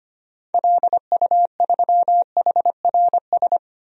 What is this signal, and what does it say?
Morse code, heard as a keyed tone, about 25 wpm, 690 Hz. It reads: LU3HRS
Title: DXCC-100 Call Signs - 6 Characters - 25wpm